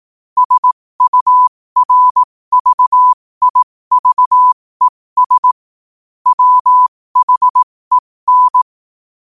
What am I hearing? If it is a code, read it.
Morse code, heard as a keyed tone, about 18 wpm, spaced as for 15 wpm, 1000 Hz. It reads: SURVIVES WHEN
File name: qotdfiles / 15wpm